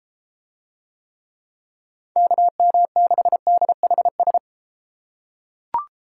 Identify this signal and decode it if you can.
Morse code, heard as a keyed tone, about 31 wpm, 700 Hz. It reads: KM6DHS